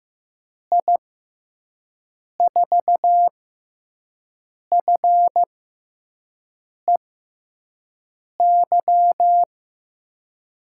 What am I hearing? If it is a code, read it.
Morse code, heard as a keyed tone, about 15 wpm, spaced as for 5 wpm, 700 Hz. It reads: I4FEY
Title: USA Call Signs - 6 Characters - 6x Character Spacing - 15wpm